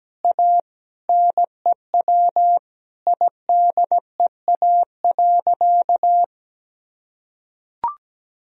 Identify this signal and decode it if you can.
Morse code, heard as a keyed tone, about 17 wpm, 700 Hz. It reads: A NEW IDEA.
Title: Sentences from Top 1000 Words - Review Part 1 - 17wpm